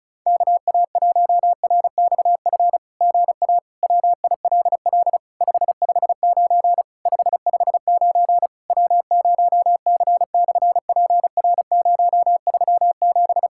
Morse code, heard as a keyed tone, about 36 wpm, 700 Hz. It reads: KA1RXF GA WILL 559 559 W0C/PR037
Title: SOTA QSO Practice 35wpm